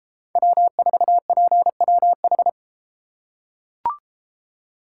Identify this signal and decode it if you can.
Morse code, heard as a keyed tone, about 32 wpm, 700 Hz. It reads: W4PWH